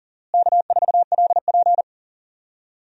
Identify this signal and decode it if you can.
Morse code, heard as a keyed tone, about 40 wpm, 700 Hz. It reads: K4LP
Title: USA Call Signs - 4 Characters - Repeated 2x - 40wpm